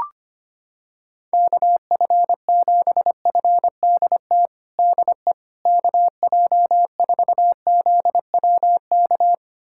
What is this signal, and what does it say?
Morse code, heard as a keyed tone, about 25 wpm, 700 Hz. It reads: KF7FDT DE KJ4ZWK
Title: USA Callsign DE Callsign - 6 Characters - 25wpm